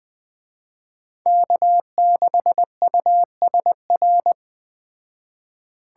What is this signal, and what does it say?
Morse code, heard as a keyed tone, about 20 wpm, 700 Hz. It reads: K6USR